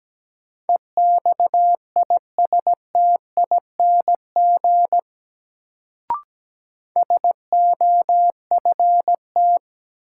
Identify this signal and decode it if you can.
Morse code, heard as a keyed tone, about 17 wpm, 700 Hz. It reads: EXISTING SOFT
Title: Sentences from Top 1500 Words - New Word Drill Alt - 17wpm